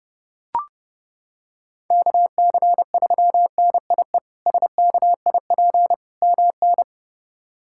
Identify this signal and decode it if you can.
Morse code, heard as a keyed tone, about 31 wpm, 700 Hz. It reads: KC3NIE SKIP MN
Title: K1USN SST QSO Practice - 30wpm